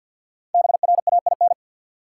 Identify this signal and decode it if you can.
Morse code, heard as a keyed tone, about 53 wpm, 700 Hz. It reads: BRAIN
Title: Sentences from Top 1000 Words - Review Part 6 - New Word Drill - 50wpm